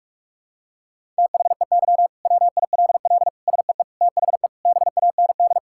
Morse code, heard as a keyed tone, about 46 wpm, 700 Hz. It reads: THEY WILL SEE THE BAND
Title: Sentences from Top 1600 Words 45wpm